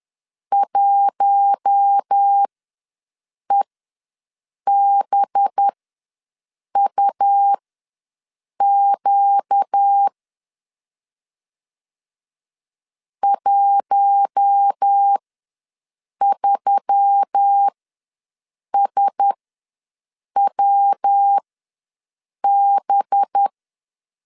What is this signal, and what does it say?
Morse code, heard as a keyed tone, about 10 wpm, spaced as for 6 wpm, 800 Hz. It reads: 1EBUQ 13SWB
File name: LEZIONI / 5 LBVC / 5 LB ws 10wpm 3616